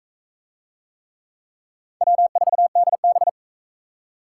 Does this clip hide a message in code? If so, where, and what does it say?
Yes: Morse code from 2.0 to 3.3 s, heard as a keyed tone, about 42 wpm, 700 Hz. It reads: W4DB